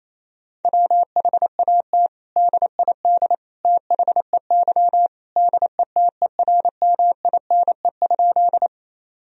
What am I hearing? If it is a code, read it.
Morse code, heard as a keyed tone, about 28 wpm, 700 Hz. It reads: WHAT DID THEY DETERMINE?
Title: Sentences from Top 1000 Words - Review Part 4 - 28wpm